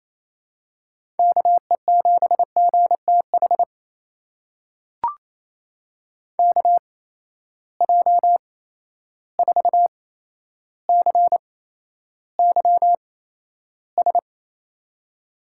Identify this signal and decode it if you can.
Morse code, heard as a keyed tone, about 28 wpm, spaced as for 12 wpm, 700 Hz. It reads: KE7GTH KJ4CYS
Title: USA Call Signs - 6 Characters - 8x Character Spacing - 28wpm